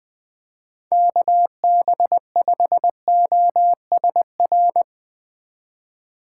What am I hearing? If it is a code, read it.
Morse code, heard as a keyed tone, about 22 wpm, 700 Hz. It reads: KB5OSR
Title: Speed Racing - USA 2x3 Call Signs 20wpm